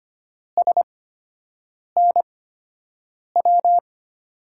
Morse code, heard as a keyed tone, about 25 wpm, spaced as for 7 wpm, 700 Hz. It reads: SNW
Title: USA Call Signs - 5 Characters - 8x Character Spacing - 25wpm